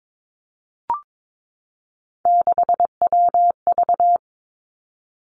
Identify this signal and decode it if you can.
Morse code, heard as a keyed tone, about 22 wpm, 700 Hz. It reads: 6WV
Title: Three Challenging Letter-Number Combinations 22wpm